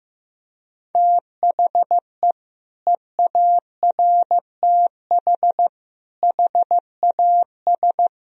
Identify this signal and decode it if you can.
Morse code, heard as a keyed tone, about 16 wpm, 700 Hz. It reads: THE EARTH HAS
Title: Sentences from Top 1000 Words - Review Part 2 - 15wpm